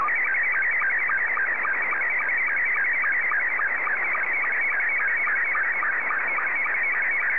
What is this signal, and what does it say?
SSTV (slow-scan television), Scottie 2.